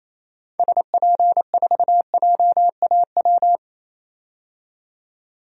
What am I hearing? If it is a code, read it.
Morse code, heard as a keyed tone, about 28 wpm, 700 Hz. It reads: SP4JAW
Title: DXCC-100 Call Signs - 6 Characters - 28wpm